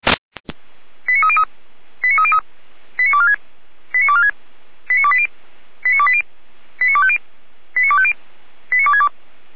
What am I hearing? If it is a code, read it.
Five-tone selective call (ZVEI), heard as quick runs of five tones, tones of about 70 ms. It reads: 89303 89303 89246 89246 89280 89280 89250 89250 89262